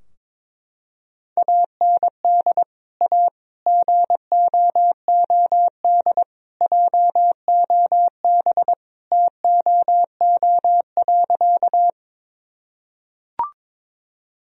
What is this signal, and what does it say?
Morse code, heard as a keyed tone, about 22 wpm, 700 Hz. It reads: AND A GOOD JOB TOO.
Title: Sentences from Top 1000 Words - Review Part 3 - 22wpm